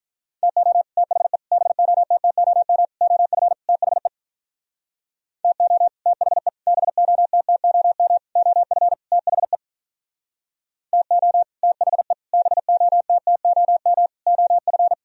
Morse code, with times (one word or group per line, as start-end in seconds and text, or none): 0.4-0.8 TO
1.0-1.4 THE
1.5-2.9 BOTTOM
3.0-3.5 OF
3.7-4.1 THE
5.4-5.9 TO
6.1-6.5 THE
6.7-8.2 BOTTOM
8.4-8.9 OF
9.1-9.6 THE
10.9-11.4 TO
11.6-12.1 THE
12.3-14.1 BOTTOM
14.3-14.9 OF